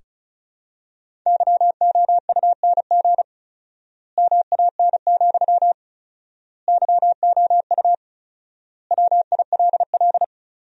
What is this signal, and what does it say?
Morse code, heard as a keyed tone, about 34 wpm, 700 Hz. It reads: YOUNG MAN, YOU WILL
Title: 4x Word-Spacing - Sentences from Top 400 Words - 35wpm